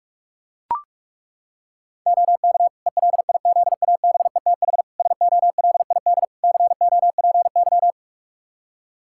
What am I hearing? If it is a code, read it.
Morse code, heard as a keyed tone, about 41 wpm, 700 Hz. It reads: OK ELIZABETH SOLID COPY
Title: Practice QSO - Exchange 3 - More Info - 45wpm